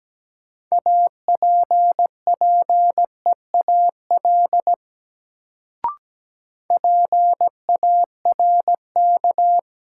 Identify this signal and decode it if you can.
Morse code, heard as a keyed tone, about 17 wpm, 700 Hz. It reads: APPEAL PARK
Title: Sentences from Top 1300 Words - New Word Drill - 17wpm